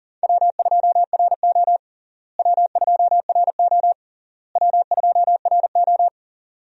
Morse code, heard as a keyed tone, about 37 wpm, 700 Hz. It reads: W2RO W2RO W2RO
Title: USA Call Signs - 4 Characters - Repeated 3x - 40wpm